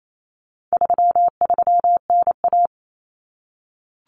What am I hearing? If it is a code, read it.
Morse code, heard as a keyed tone, about 28 wpm, 700 Hz. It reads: 33NA
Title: Common Cut Numbers - 4-Digit 28wpm